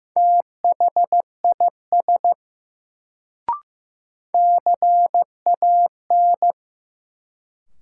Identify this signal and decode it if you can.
Morse code, heard as a keyed tone, about 15 wpm, 700 Hz. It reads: THIS CAN
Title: Sentences from Top 100 Words - New Word Drill - 15wpm